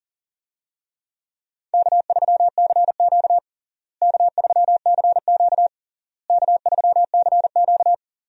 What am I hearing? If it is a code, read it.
Morse code, heard as a keyed tone, about 40 wpm, 700 Hz. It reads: K3CQ K3CQ K3CQ
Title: USA Call Signs - 4 Characters - Repeated 3x - 40wpm